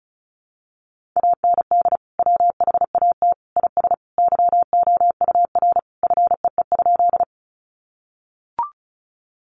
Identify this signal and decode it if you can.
Morse code, heard as a keyed tone, about 34 wpm, 700 Hz. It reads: AND WHAT IS YOUR FEE?
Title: Sentences from Top 1500 Words 35wpm